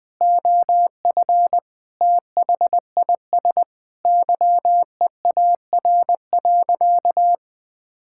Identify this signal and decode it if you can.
Morse code, heard as a keyed tone, about 20 wpm, 700 Hz. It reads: OF THIS YEAR.
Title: Sentences with 8-Word Phrases - 20wpm